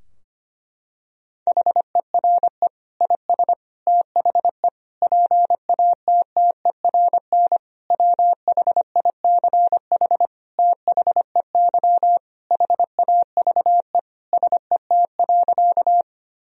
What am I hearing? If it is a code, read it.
Morse code, heard as a keyed tone, about 25 wpm, 700 Hz. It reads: HERE IS THE PATTERN WHICH THEY HAVE SET.
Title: Sentences from Top 1000 Words - Review Part 2 - 25wpm